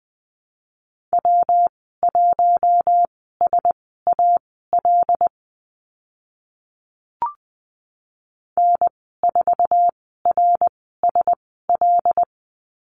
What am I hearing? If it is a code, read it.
Morse code, heard as a keyed tone, about 20 wpm, spaced as for 15 wpm, 700 Hz. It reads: W1SAL N4RSL
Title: 147 Lesson 24g - Letter y - Call Signs 20wpm